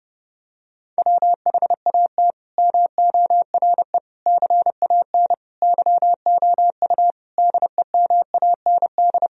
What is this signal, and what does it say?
Morse code, heard as a keyed tone, about 30 wpm, 700 Hz. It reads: WHAT MORE CAN YOU DEMAND
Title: Sentences from Top 1000 Words - Review Part 4 - 30wpm